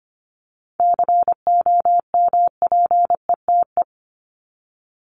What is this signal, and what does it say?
Morse code, heard as a keyed tone, about 25 wpm, 700 Hz. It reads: COMPETE
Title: Top 1751-2000 Words 25wpm